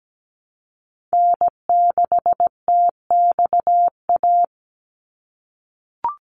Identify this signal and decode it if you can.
Morse code, heard as a keyed tone, about 17 wpm, 700 Hz. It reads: N6TXA
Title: USA Call Signs 17wpm